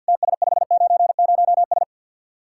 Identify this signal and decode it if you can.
Morse code, heard as a keyed tone, about 50 wpm, 700 Hz. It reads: TS590S
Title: Speed-Racing Common QSO Phrases - 50wpm